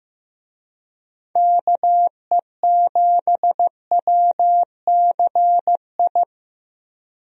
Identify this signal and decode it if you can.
Morse code, heard as a keyed tone, about 16 wpm, 700 Hz. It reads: KE7WCI
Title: USA Call Signs - 6 Characters - Repeated 3x - 15wpm